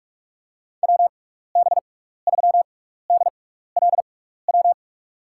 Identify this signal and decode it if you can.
Morse code, heard as a keyed tone, about 44 wpm, spaced as for 15 wpm, 700 Hz. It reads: WB3DLW
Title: USA Call Signs - 6 Characters - 6x Character Spacing - 45wpm